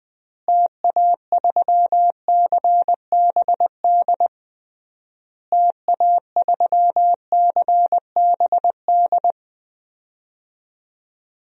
Morse code, heard as a keyed tone, about 20 wpm, 700 Hz. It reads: TA3CBD TA3CBD
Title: DXCC-100 Call Signs - 6 Characters - Repeated 3x - 20wpm